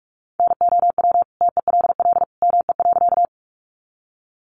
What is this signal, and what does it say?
Morse code, heard as a keyed tone, about 44 wpm, 700 Hz. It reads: NOW TELL ME.
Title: Sentences from Top 200 Words 45wpm